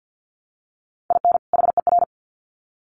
Morse code, heard as a keyed tone, about 50 wpm, 700 Hz. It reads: IN HER
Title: Sets of 2 Words 50wpm